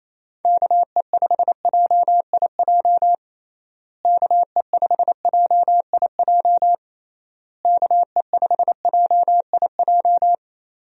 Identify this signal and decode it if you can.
Morse code, heard as a keyed tone, about 27 wpm, 700 Hz. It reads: KE5JIJ KE5JIJ KE5JIJ
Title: USA Call Signs - 6 Characters - Repeated 3x - 28wpm